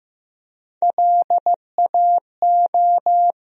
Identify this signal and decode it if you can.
Morse code, heard as a keyed tone, about 15 wpm, 700 Hz. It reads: LAO
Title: USA Call Sign Postfixes - 3 Letters 15wpm